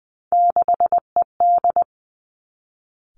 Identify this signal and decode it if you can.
Morse code, heard as a keyed tone, about 20 wpm, 700 Hz. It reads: 6ED
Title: Three Challenging Letter-Number Combinations 20wpm